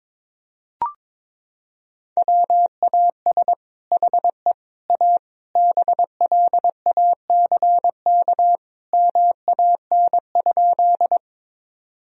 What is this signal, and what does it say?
Morse code, heard as a keyed tone, about 22 wpm, 700 Hz. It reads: WAS HE A BLACK MAN?
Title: Sentences from Top 1000 Words - Review Part 2 - 22wpm